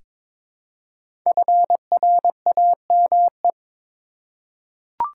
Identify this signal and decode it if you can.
Morse code, heard as a keyed tone, about 22 wpm, 700 Hz. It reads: FRAME
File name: 5 Letter Words - First 3 Letters Given - 22wpm